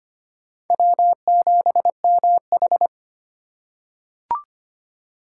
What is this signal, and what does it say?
Morse code, heard as a keyed tone, about 25 wpm, 700 Hz. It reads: W7MH
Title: USA Call Signs - 4 Characters - 4x Character Spacing - 25wpm